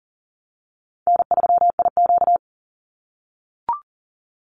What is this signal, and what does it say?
Morse code, heard as a keyed tone, about 40 wpm, 700 Hz. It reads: N3IQ
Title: Easy Call Signs - 40wpm